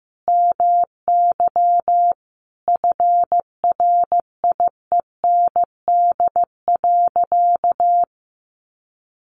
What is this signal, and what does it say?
Morse code, heard as a keyed tone, about 15 wpm, 700 Hz. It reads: MY FRIEND.
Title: Sentences from Top 1300 Words 15wpm